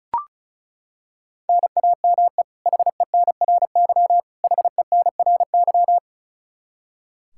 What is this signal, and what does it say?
Morse code, heard as a keyed tone, about 36 wpm, 700 Hz. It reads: NAME HENRY HENRY
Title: Practice QSO - Exchange 1 - Beginning - 35wpm